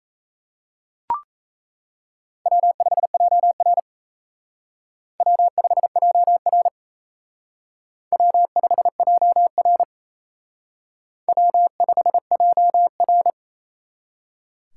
Morse code, heard as a keyed tone, about 35 wpm, 700 Hz. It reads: W5JR W5JR W5JR W5JR